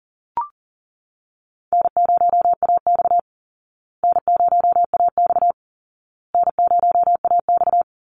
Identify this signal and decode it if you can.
Morse code, heard as a keyed tone, about 37 wpm, 700 Hz. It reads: N0AX N0AX N0AX
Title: Call Signs Repeated 2x - 40wpm